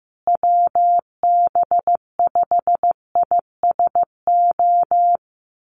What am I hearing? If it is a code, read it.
Morse code, heard as a keyed tone, about 15 wpm, 700 Hz. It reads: WB5ISO
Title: USA Call Signs Repeated 3x 15wpm